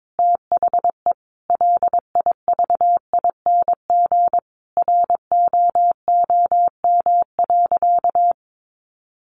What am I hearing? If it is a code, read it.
Morse code, heard as a keyed tone, about 22 wpm, 700 Hz. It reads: THE LIVING ROOM.
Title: Sentences from Top 1500 Words 22wpm